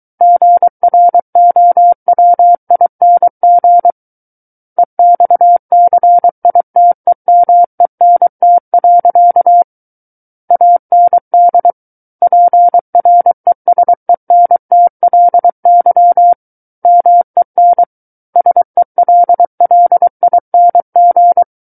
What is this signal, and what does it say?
Morse code, heard as a keyed tone, about 23 wpm, 700 Hz. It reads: GROWING EXCITEMENT. AND PRESENTLY MEN SELLING
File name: Book One / 14-Chapter 14 - 23 WPM